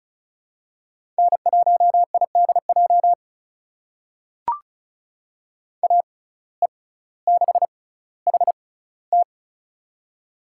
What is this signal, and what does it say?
Morse code, heard as a keyed tone, about 36 wpm, spaced as for 18 wpm, 700 Hz. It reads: N1IDJ AE6HT
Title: USA Call Signs - 5 Characters - 6x Character Spacing - 35wpm